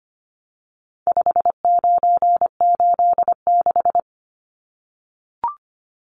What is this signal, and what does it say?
Morse code, heard as a keyed tone, about 25 wpm, 700 Hz. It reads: 5986
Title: Numbers - 4-Digit 25wpm